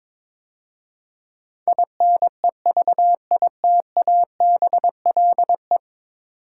Morse code, heard as a keyed tone, about 22 wpm, 700 Hz. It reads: INEVITABLE
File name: practice / 22wpm / 10 Letter Words - First 3 Letters Given - 22wpm